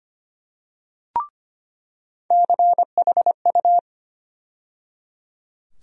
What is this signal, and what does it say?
Morse code, heard as a keyed tone, about 25 wpm, 700 Hz. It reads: CHU
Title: USA Call Sign Postfixes - 3 Letters 25wpm